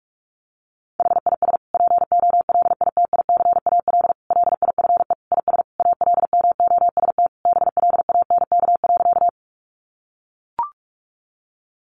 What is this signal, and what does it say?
Morse code, heard as a keyed tone, about 44 wpm, 700 Hz. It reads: HIS POLITICAL LIFE IS ALMOST BLANK.
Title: Sentences from Top 1500 Words 45wpm